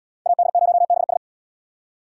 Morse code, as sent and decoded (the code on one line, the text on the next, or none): .- -.. ----- -... ...
AD0BS